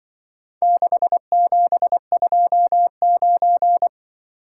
Morse code, heard as a keyed tone, about 24 wpm, 700 Hz. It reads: 6729
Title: Speed-Racing - Four Letter-Number Combinations - 20wpm